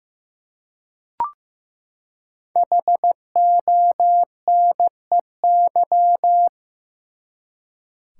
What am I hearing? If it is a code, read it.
Morse code, heard as a keyed tone, about 15 wpm, 700 Hz. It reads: HONEY
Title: Five Letter Words - 15wpm